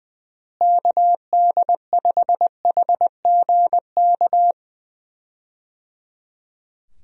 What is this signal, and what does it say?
Morse code, heard as a keyed tone, about 20 wpm, 700 Hz. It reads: KD5HGK
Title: USA Call Signs - 6 Characters - Repeated 3x - 20wpm